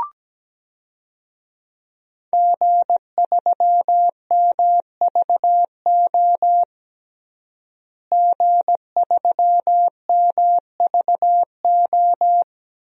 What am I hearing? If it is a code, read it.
Morse code, heard as a keyed tone, about 17 wpm, 700 Hz. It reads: G3MVO G3MVO
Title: DXCC-100 Call Signs - 5 Characters - Repeated 2x - 17wpm